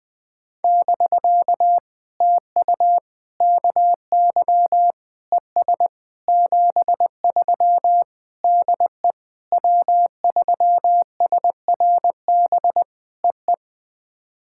Morse code, heard as a keyed tone, about 21 wpm, 700 Hz. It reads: <BK> TU KY ES 73 DE W3SRB EE
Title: POTA QSO Practice - 20wpm